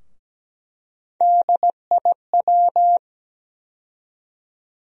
Morse code, heard as a keyed tone, about 17 wpm, 700 Hz. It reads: DIW
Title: USA Call Sign Postfixes - 3 Letters 17wpm